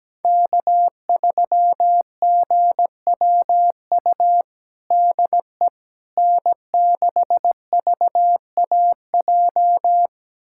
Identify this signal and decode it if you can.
Morse code, heard as a keyed tone, about 17 wpm, 700 Hz. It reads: K3GWU DE N6VAJ
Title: USA Callsign DE Callsign - 5 Characters - 17wpm